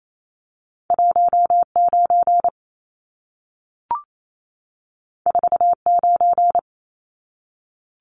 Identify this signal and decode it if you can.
Morse code, heard as a keyed tone, about 28 wpm, 700 Hz. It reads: 19 49
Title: Numbers - 2-Digit 28wpm